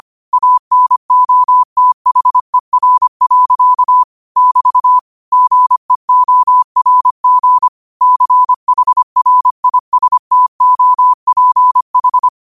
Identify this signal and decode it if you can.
Morse code, heard as a keyed tone, about 25 wpm, 1000 Hz. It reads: ANOTHER. <BT> GEORG CHRISTOPH
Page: New quotes every day in morse code at 25 Words per minute.